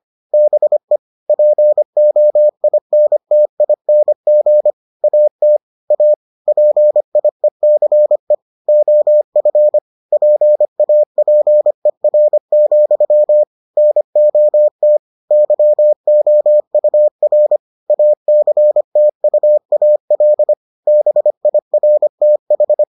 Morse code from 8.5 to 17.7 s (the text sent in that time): OF PAPER, NOT YOUR